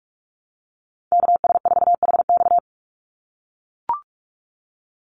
Morse code, heard as a keyed tone, about 44 wpm, 700 Hz. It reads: KS4HX
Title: USA Call Signs 45wpm